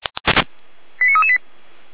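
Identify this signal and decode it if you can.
Five-tone selective call (ZVEI), heard as quick runs of five tones, tones of about 70 ms. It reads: 89338
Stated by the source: ZVEI five-tone selective call received on SDR.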